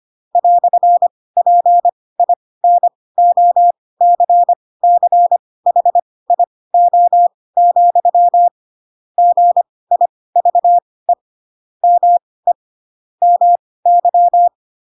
Morse code, heard as a keyed tone, about 25 wpm, spaced as for 18 wpm, 700 Hz. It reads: "PINOCCHIO, GIVE ME MY
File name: Pinocchio ch3 12 25 ik1hge